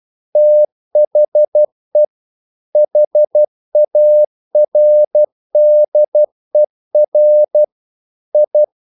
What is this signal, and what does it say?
Morse code, heard as a keyed tone, about 12 wpm, 600 Hz. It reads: THE HARDER I
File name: CW / Practice / EnglishQuotations0112